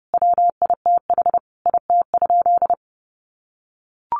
Morse code, heard as a keyed tone, about 30 wpm, 700 Hz. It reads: WITH IT?
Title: Sentences from Top 1400 Words 30wpm